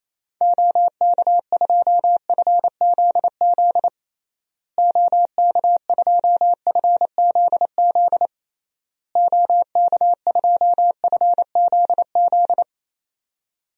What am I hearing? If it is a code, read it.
Morse code, heard as a keyed tone, about 28 wpm, 700 Hz. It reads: OK2FZZ OK2FZZ OK2FZZ
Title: DXCC-100 Call Signs - 6 Characters - Repeated 3x - 28wpm